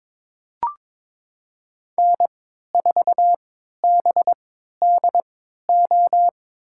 Morse code, heard as a keyed tone, about 22 wpm, spaced as for 13 wpm, 700 Hz. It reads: N4BDO
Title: USA Call Signs - 3x Character Spacing - 22wpm